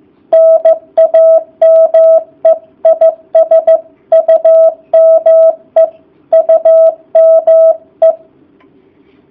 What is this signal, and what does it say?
Morse code, heard as a keyed tone, about 15 wpm, 650 Hz. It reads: NAME IS UME UME